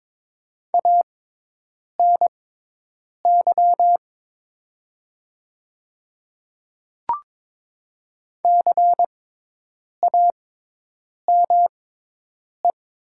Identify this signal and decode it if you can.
Morse code, heard as a keyed tone, about 22 wpm, spaced as for 8 wpm, 700 Hz. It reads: ANY CAME
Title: 6x Character Spacing - Top 200 Words - 22wpm